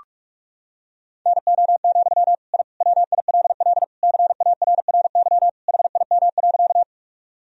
Morse code, heard as a keyed tone, about 45 wpm, 700 Hz. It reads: NO, I WILL CARRY HIM.
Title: Sentences from Top 1000 Words - Review Part 1 - 45wpm